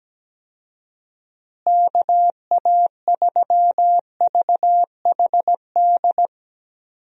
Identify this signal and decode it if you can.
Morse code, heard as a keyed tone, about 17 wpm, 700 Hz. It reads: KA3VHD
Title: USA Call Signs - 6 Characters - Repeated 2x - 17wpm